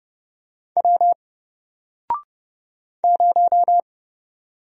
Morse code, heard as a keyed tone, about 30 wpm, 700 Hz. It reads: W 0
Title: Single Letter-Number Alt 30wpm